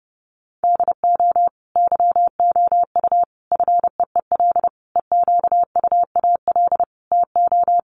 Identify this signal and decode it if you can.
Morse code, heard as a keyed tone, about 30 wpm, 700 Hz. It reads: DO YOU FEEL EQUAL TO
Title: Sentences from Top 1400 Words 30wpm